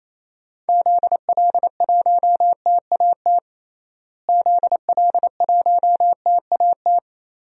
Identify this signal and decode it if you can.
Morse code, heard as a keyed tone, about 28 wpm, 700 Hz. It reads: ZL1TAT ZL1TAT
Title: DXCC-100 Call Signs - 6 Characters - Repeated 3x - 28wpm